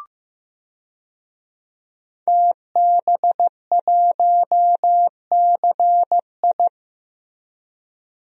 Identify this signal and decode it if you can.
Morse code, heard as a keyed tone, about 16 wpm, 700 Hz. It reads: TB1CI
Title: DXCC-100 Call Signs - 5 Characters - Repeated 3x - 15wpm